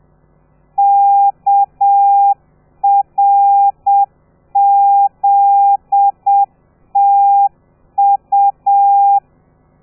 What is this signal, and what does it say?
Morse code, heard as a keyed tone, about 7 wpm, 800 Hz. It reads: KRZTU
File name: telegrafia / i2tfj / Lez13 30